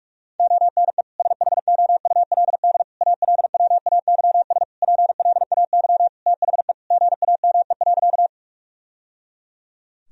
Morse code, heard as a keyed tone, about 46 wpm, 700 Hz. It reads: ONE SHOULD ALWAYS PLAY THE GAME.
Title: Sentences from Top 1000 Words - Review Part 3 - 45wpm